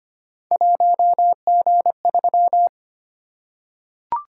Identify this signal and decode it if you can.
Morse code, heard as a keyed tone, about 25 wpm, 700 Hz. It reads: 1G3
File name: Three Letter-Number Combinations - 25wpm